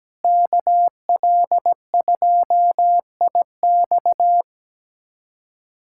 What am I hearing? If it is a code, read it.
Morse code, heard as a keyed tone, about 17 wpm, 700 Hz. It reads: KL2IX
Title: USA Callsign DE Callsign - 5 Characters - 17wpm